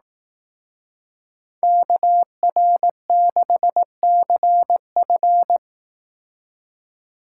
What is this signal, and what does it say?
Morse code, heard as a keyed tone, about 18 wpm, 700 Hz. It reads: KR6CF
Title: Speed Racing - USA Call Signs 15wpm